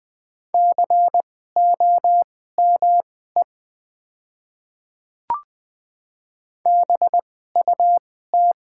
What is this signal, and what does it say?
Morse code, heard as a keyed tone, about 20 wpm, spaced as for 15 wpm, 700 Hz. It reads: COME BUT
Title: 2x Character Spacing - Top 100 Words - 20wpm